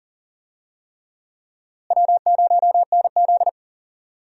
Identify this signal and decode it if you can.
Morse code, heard as a keyed tone, about 37 wpm, 700 Hz. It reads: W0NZ
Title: USA Call Signs - 4 Characters - Repeated 3x - 40wpm